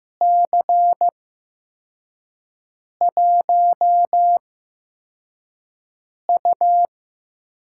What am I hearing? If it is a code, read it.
Morse code, heard as a keyed tone, about 15 wpm, spaced as for 4 wpm, 700 Hz. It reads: C1U
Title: USA Call Signs - 4 Characters - 8x Character Spacing - 15wpm